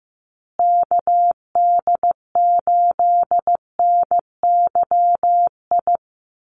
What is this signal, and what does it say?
Morse code, heard as a keyed tone, about 15 wpm, 700 Hz. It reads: KD8NYI
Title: USA Call Signs Repeated 3x 15wpm